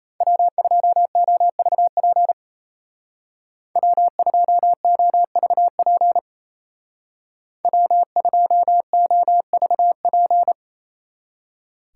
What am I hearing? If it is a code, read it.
Morse code, heard as a keyed tone, about 33 wpm, 700 Hz. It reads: W2OVP W2OVP W2OVP
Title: Speed Racing - USA 1x3 Call Signs 28wpm